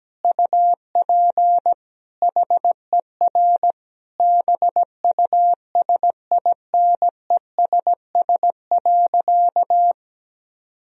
Morse code, heard as a keyed tone, about 17 wpm, 700 Hz. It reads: UP HER BUSINESS.
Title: Sentences from Top 1000 Words - Review Part 1 - 17wpm